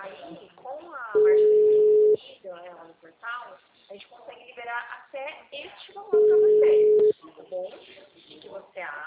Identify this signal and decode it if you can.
Ringback tone (Europe, 425 Hz), with speech.